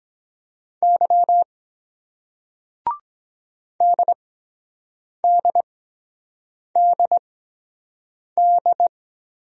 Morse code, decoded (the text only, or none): Y DDDD